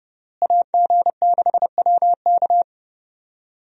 Morse code, heard as a keyed tone, about 32 wpm, 700 Hz. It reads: AG6WK
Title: USA Call Signs - 5 Characters - Repeated 3x - 30wpm